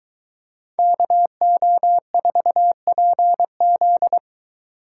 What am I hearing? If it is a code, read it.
Morse code, heard as a keyed tone, about 21 wpm, 700 Hz. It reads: KO4PZ